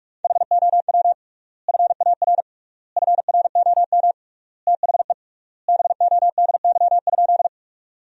Morse code, heard as a keyed tone, about 46 wpm, 700 Hz. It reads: HOW FAR FROM THE BODY?
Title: 3x Word-Spacing - Sentences from Top 400 Words - 45wpm